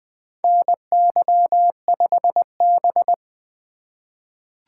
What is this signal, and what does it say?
Morse code, heard as a keyed tone, about 21 wpm, 700 Hz. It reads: NY5B
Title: USA Call Signs - 4 Characters - 3x Character Spacing - 20wpm